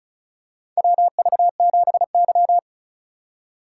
Morse code, heard as a keyed tone, about 36 wpm, 700 Hz. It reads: WV7Y
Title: USA Call Signs - 4 Characters - 4x Character Spacing - 35wpm